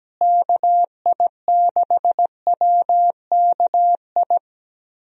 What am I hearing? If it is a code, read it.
Morse code, heard as a keyed tone, about 17 wpm, 700 Hz. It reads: KI6WKI
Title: USA Call Signs - 6 Characters - Repeated 3x - 17wpm